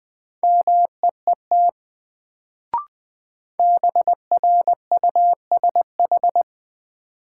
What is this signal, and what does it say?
Morse code, heard as a keyed tone, about 20 wpm, 700 Hz. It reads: MEET BRUSH
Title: Sentences from Top 1100 Words - New Word Drill Alt - 20wpm